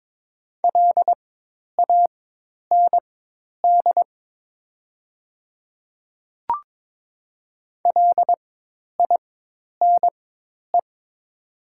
Morse code, heard as a keyed tone, about 22 wpm, spaced as for 10 wpm, 700 Hz. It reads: LAND LINE
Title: 4x Character Spacing - Top 200 Words - 22wpm